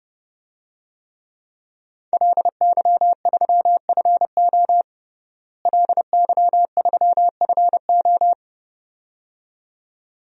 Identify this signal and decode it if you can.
Morse code, heard as a keyed tone, about 30 wpm, 700 Hz. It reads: LY3FO LY3FO
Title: DXCC-100 Call Signs - 5 Characters - Repeated 2x - 30wpm